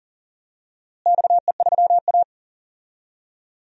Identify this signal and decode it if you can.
Morse code, heard as a keyed tone, about 40 wpm, 700 Hz. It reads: XE3A
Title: DXCC-100 Call Signs - 4 Characters - Repeated 2x - 40wpm